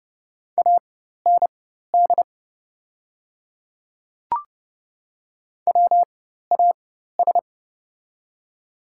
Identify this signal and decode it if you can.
Morse code, heard as a keyed tone, about 30 wpm, spaced as for 14 wpm, 700 Hz. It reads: AND WAS
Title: 4x Character Spacing - Top 100 Words - 30wpm